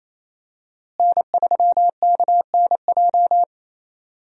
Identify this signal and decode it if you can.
Morse code, heard as a keyed tone, about 28 wpm, 700 Hz. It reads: N3KNJ
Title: USA Call Signs - 5 Characters - Repeated 2x - 28wpm